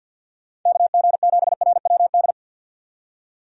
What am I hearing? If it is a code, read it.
Morse code, heard as a keyed tone, about 51 wpm, 700 Hz. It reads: KG7RWD
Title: USA Call Signs - 6 Characters - 50wpm